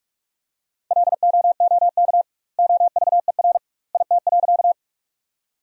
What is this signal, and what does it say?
Morse code, heard as a keyed tone, about 46 wpm, 700 Hz. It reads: LOOK OVER IT.
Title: 2x Word-Spacing - Sentences from Top 100 Words - 45wpm